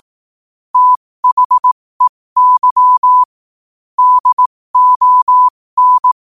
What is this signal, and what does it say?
Morse code, heard as a keyed tone, about 18 wpm, spaced as for 15 wpm, 1000 Hz. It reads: THEY DON
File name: 15wpm